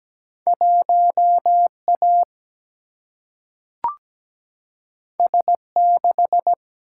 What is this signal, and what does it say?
Morse code, heard as a keyed tone, about 17 wpm, 700 Hz. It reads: 1A S6
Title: Single-Letter Single-Number 17wpm